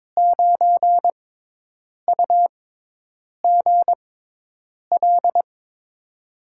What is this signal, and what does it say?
Morse code, heard as a keyed tone, about 22 wpm, spaced as for 8 wpm, 700 Hz. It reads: 9UGL
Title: USA Call Signs - 6 Characters - 6x Character Spacing - 22wpm